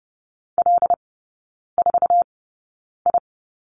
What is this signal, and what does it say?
Morse code, heard as a keyed tone, about 30 wpm, spaced as for 9 wpm, 700 Hz. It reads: L4I
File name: Lesson 17e - Letter u - Three Characters 30wpm